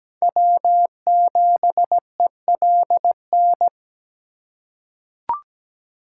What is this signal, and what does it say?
Morse code, heard as a keyed tone, about 17 wpm, 700 Hz. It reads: W7ELN
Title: USA Call Signs - 5 Characters - 3x Character Spacing - 17wpm